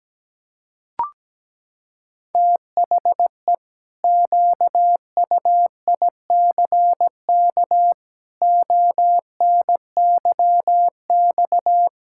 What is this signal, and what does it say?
Morse code, heard as a keyed tone, about 17 wpm, 700 Hz. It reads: THE QUICK ONYX
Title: Pangram Challenge v2 - 17wpm